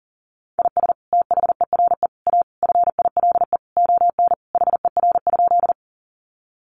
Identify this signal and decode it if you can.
Morse code, heard as a keyed tone, about 37 wpm, 700 Hz. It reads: IS THERE A FILE ON HER?
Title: Sentences from Top 1200 Words 40wpm